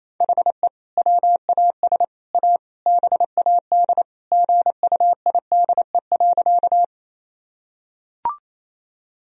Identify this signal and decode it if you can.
Morse code, heard as a keyed tone, about 28 wpm, 700 Hz. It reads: HE WAS A BAD GUIDE.
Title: Sentences from Top 1000 Words - Review Part 3 - 28wpm